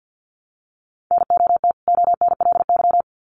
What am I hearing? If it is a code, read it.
Morse code, heard as a keyed tone, about 48 wpm, 700 Hz. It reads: NOT ONLY
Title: Sets of 2 Words 50wpm